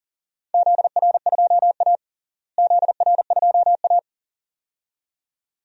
Morse code, heard as a keyed tone, about 40 wpm, 700 Hz. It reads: ZR2A ZR2A
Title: DXCC-100 Call Signs - 4 Characters - Repeated 2x - 40wpm